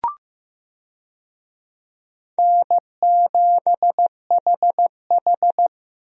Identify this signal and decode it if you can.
Morse code, heard as a keyed tone, about 15 wpm, 700 Hz. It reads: N7HH
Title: USA Call Signs - 4 Characters - Repeated 2x - 15wpm